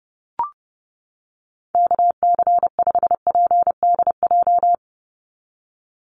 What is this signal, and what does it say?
Morse code, heard as a keyed tone, about 30 wpm, 700 Hz. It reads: KC5PDJ
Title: USA Call Signs 30wpm